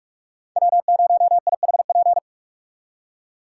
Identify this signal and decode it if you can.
Morse code, heard as a keyed tone, about 62 wpm, spaced as for 53 wpm, 700 Hz. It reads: W0IHP
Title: Speed Racing - USA 1x3 Call Signs 45wpm